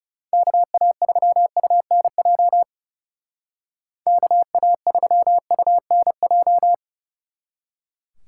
Morse code, heard as a keyed tone, about 39 wpm, 700 Hz. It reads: KA3UNJ KA3UNJ